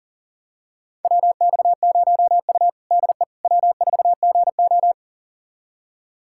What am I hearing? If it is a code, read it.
Morse code, heard as a keyed tone, about 40 wpm, 700 Hz. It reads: WX0U DE W4GO
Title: USA Callsign DE Callsign - 4 Characters - 40wpm